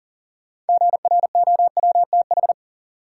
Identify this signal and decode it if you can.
Morse code, heard as a keyed tone, about 39 wpm, 700 Hz. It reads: GROWTH
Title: Sentences from Top 1000 Words - Review Part 5 - New Word Drill - 40wpm